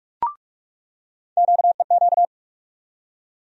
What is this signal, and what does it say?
Morse code, heard as a keyed tone, about 43 wpm, 700 Hz. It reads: QEQ
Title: USA Call Sign Postfixes - 3 Letters 45wpm